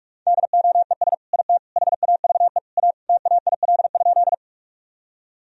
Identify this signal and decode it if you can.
Morse code, heard as a keyed tone, about 47 wpm, 700 Hz. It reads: DOES IT HAVE A TAIL?
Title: Sentences from Top 1000 Words - Review Part 3 - 45wpm